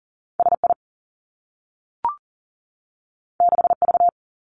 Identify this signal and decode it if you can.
Morse code, heard as a keyed tone, about 40 wpm, 700 Hz. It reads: S I 6 V
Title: Two Challenging Letter-Number Combinations 40wpm